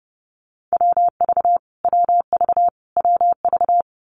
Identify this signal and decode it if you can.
Morse code, heard as a keyed tone, about 30 wpm, 700 Hz. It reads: WV WV WV
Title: US States Repeated 3x 30wpm